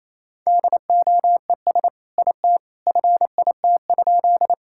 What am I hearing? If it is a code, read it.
Morse code, heard as a keyed tone, about 28 wpm, 700 Hz. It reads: DOES IT FIT?
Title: Sentences from Top 1000 Words - Review Part 5 - 28wpm